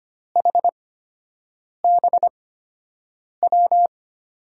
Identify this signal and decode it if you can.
Morse code, heard as a keyed tone, about 25 wpm, spaced as for 7 wpm, 700 Hz. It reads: HBW
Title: USA Call Signs - 5 Characters - 8x Character Spacing - 25wpm